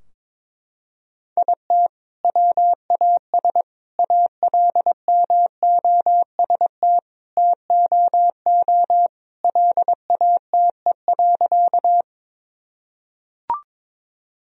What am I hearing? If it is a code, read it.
Morse code, heard as a keyed tone, about 22 wpm, 700 Hz. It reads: IT WAS ALMOST TOO LATE.
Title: Sentences from Top 1000 Words - Review Part 4 - 22wpm